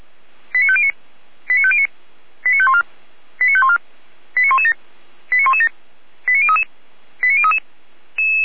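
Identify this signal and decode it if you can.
Five-tone selective call (ZVEI), heard as quick runs of five tones, tones of about 70 ms. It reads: 89509 89509 89514 89514 89117 89117 89033 89033
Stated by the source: ZVEI five-tone selective call received on an SDR.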